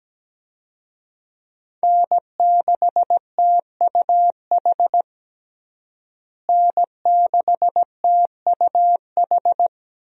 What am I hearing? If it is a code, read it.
Morse code, heard as a keyed tone, about 17 wpm, 700 Hz. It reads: N6TUH N6TUH
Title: USA Call Signs - 5 Characters - Repeated 3x - 17wpm